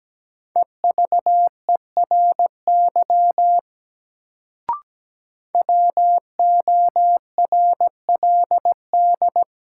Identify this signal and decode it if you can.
Morse code, heard as a keyed tone, about 17 wpm, 700 Hz. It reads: EVERY WORLD
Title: Sentences from Top 200 Words - New Word Drill - 17wpm